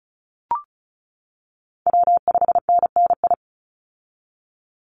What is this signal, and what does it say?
Morse code, heard as a keyed tone, about 36 wpm, 700 Hz. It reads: W5NNI
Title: USA Call Signs 35wpm